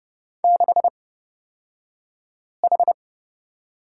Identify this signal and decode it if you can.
Morse code, heard as a keyed tone, about 31 wpm, spaced as for 5 wpm, 700 Hz. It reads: 6H
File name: Lesson 36d - BT - Two Characters 30wpm